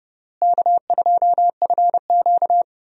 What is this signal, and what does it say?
Morse code, heard as a keyed tone, about 30 wpm, 700 Hz. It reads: K2FQ
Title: USA Call Signs - 4 Characters - Repeated 3x - 30wpm